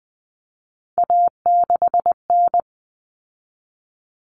Morse code, heard as a keyed tone, about 20 wpm, 700 Hz. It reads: A6N
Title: All Cut Numbers - 3-Digit 20wpm